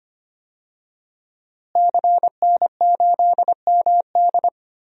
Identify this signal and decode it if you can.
Morse code, heard as a keyed tone, about 25 wpm, 700 Hz. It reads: CN8MD
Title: DXCC-100 Call Signs - 5 Characters - Repeated 2x - 25wpm